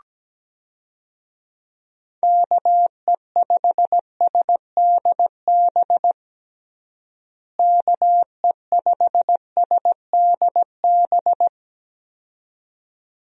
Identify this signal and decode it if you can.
Morse code, heard as a keyed tone, about 17 wpm, 700 Hz. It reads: KE5SDB KE5SDB
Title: USA Call Signs - 6 Characters - Repeated 2x - 17wpm